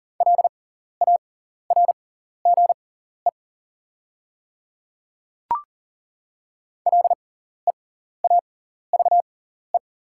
Morse code, heard as a keyed tone, about 40 wpm, spaced as for 14 wpm, 690 Hz. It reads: LARGE LEAVE
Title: 6x Character Spacing - Top 500 Words - 40wpm